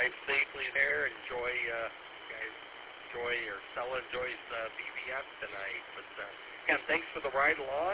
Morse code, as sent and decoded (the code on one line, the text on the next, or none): none